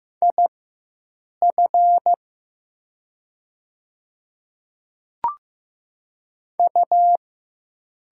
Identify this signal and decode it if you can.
Morse code, heard as a keyed tone, about 15 wpm, spaced as for 7 wpm, 700 Hz. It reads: IF U
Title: 4x Character Spacing - Top 500 Words - 15wpm